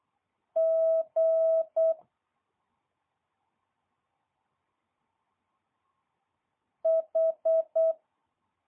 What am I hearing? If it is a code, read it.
Morse code, heard as a keyed tone, about 8 wpm, 650 Hz. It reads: G H
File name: jh8cbh / 2022cw